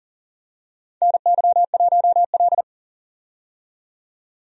Morse code, heard as a keyed tone, about 40 wpm, 700 Hz. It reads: NY1L
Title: USA Call Signs - 4 Characters - 40wpm